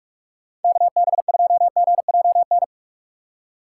Morse code, heard as a keyed tone, about 44 wpm, 700 Hz. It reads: KB2GJN
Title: USA Call Signs - 6 Characters - 6x Character Spacing - 45wpm